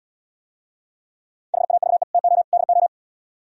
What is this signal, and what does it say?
Morse code, heard as a keyed tone, about 74 wpm, 700 Hz. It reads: HAVE TO DO